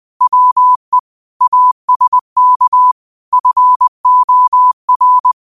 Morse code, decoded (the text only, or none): WE ASK FOR